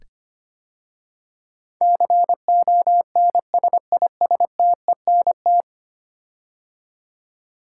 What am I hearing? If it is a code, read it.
Morse code, heard as a keyed tone, about 25 wpm, 700 Hz. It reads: CONSISTENT
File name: 10 Letter Words - First 3 Letters Given - 25wpm